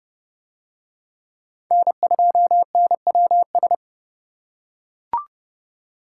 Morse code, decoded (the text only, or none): N2NWS